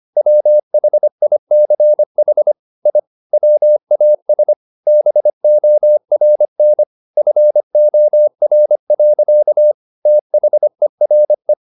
Morse code, heard as a keyed tone, about 25 wpm, 600 Hz. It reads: WHICH I WAS BORN FOR. THERE
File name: the old man and the sea part 1 25wpm